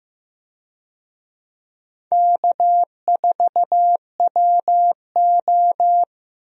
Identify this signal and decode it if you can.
Morse code, heard as a keyed tone, about 15 wpm, 700 Hz. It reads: K4WO
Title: USA Call Signs - 4 Characters - Repeated 2x - 15wpm